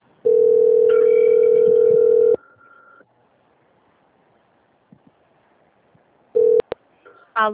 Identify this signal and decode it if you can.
Ringback tone (North America), 440 + 480 Hz.